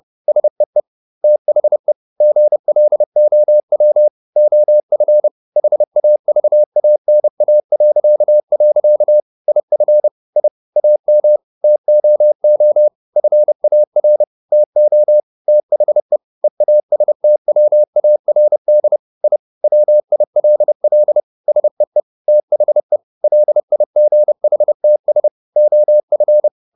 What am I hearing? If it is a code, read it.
Morse code, heard as a keyed tone, about 30 wpm, 600 Hz. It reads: SEE THE GLOW OF HAVANA.. IF I AM TOO FAR TO THE EASTWARD I WILL SEE THE LIGHTS OF